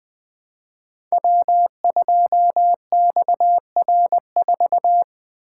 Morse code, heard as a keyed tone, about 20 wpm, 700 Hz. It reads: W2XR4
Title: Code Groups - 5 Letter:Numbers - 20wpm